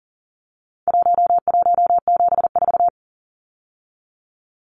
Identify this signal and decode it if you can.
Morse code, heard as a keyed tone, about 40 wpm, 700 Hz. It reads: 1174
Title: Numbers - Contest Style - 4-Digit 40wpm